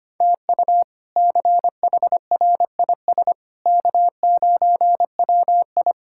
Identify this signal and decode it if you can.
Morse code, heard as a keyed tone, about 25 wpm, 700 Hz. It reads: TU CHRIS K9WI
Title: K1USN SST QSO Practice - 25wpm